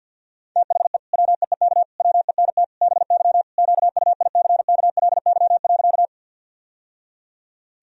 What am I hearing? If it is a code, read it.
Morse code, heard as a keyed tone, about 52 wpm, 700 Hz. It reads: THE WEEK WENT BY QUICKLY.
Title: Sentences from Top 1000 Words - Review Part 2 - 50wpm